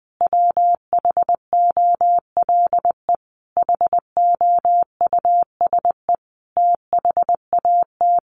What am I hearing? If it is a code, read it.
Morse code, heard as a keyed tone, about 20 wpm, 700 Hz. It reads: WHOLE HOUSE THAT
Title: Sentences from Top 1200 Words 20wpm